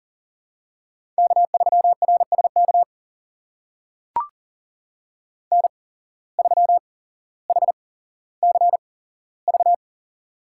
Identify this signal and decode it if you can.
Morse code, heard as a keyed tone, about 40 wpm, spaced as for 17 wpm, 700 Hz. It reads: K3RSK N3HCV
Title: USA Call Signs - 5 Characters - 8x Character Spacing - 40wpm